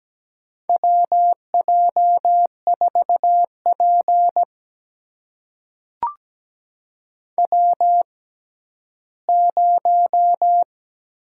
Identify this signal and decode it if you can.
Morse code, heard as a keyed tone, about 17 wpm, spaced as for 11 wpm, 700 Hz. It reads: WJ4P W0
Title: USA Call Signs - 6x Character Spacing - 17wpm